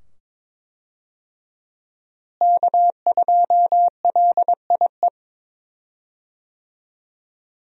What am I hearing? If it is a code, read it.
Morse code, heard as a keyed tone, about 23 wpm, 700 Hz. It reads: K2LIE
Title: USA Call Signs - 5 Characters - Repeated 2x - 22wpm